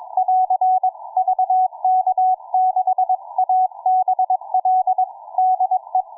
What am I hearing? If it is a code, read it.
Morse code, heard as a keyed tone, about 22 wpm, 740 Hz. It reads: <AR> VK6ABL DE